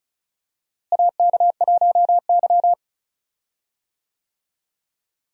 Morse code, heard as a keyed tone, about 36 wpm, 700 Hz. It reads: AK1Y
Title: USA Call Signs - 4 Characters - Repeated 3x - 35wpm